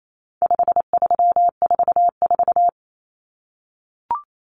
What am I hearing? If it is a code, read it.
Morse code, heard as a keyed tone, about 28 wpm, 700 Hz. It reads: 5344
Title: Numbers - 4-Digit 28wpm